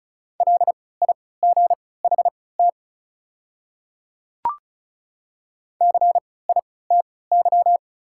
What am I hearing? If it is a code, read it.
Morse code, heard as a keyed tone, about 36 wpm, spaced as for 20 wpm, 700 Hz. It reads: LIGHT CITY
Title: 3x Character Spacing - Top 300 Words - 35wpm